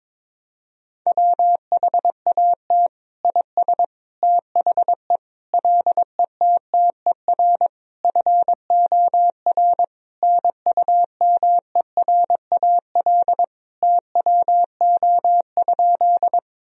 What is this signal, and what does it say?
Morse code, heard as a keyed tone, about 22 wpm, 700 Hz. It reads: WHAT IS THE LETTER FOR NUMERAL TWO?
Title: Sentences from Top 1000 Words - Review Part 2 - 22wpm